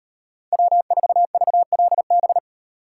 Morse code, heard as a keyed tone, about 38 wpm, 700 Hz. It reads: W4VLB